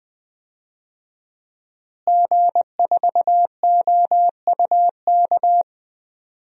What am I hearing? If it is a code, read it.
Morse code, heard as a keyed tone, about 20 wpm, 700 Hz. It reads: G4OUK
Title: DXCC-100 Call Signs - 5 Characters - Repeated 3x - 20wpm